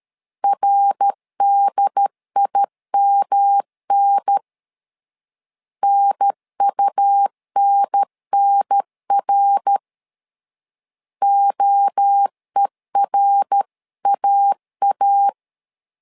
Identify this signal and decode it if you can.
Morse code, heard as a keyed tone, about 13 wpm, 800 Hz. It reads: RDIMN NUNNR OERAA
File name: LEZIONI / 3 MWRO / 3 MWRO 12wpm 1730